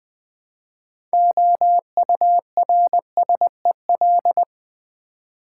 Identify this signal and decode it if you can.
Morse code, heard as a keyed tone, about 20 wpm, 700 Hz. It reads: OURSEL
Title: Build A Word - 3 Letters At A Time - 9-Letter Words - 20wpm